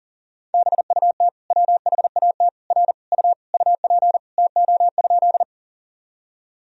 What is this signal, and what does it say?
Morse code, heard as a keyed tone, about 37 wpm, 700 Hz. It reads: BUT WHAT R U UP TO?
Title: Sentences from Top 100 Words and Abbreviations 40wpm